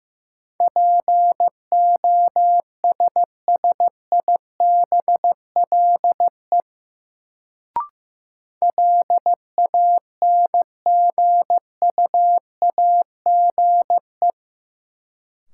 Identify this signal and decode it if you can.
Morse code, heard as a keyed tone, about 15 wpm, 700 Hz. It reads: POSSIBLE LANGUAGE
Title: Sentences from Top 500 Words - New Word Drill - 15wpm